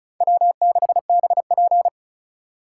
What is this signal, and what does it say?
Morse code, heard as a keyed tone, about 36 wpm, 700 Hz. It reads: W6BP
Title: USA Call Signs - 4 Characters - 3x Character Spacing - 35wpm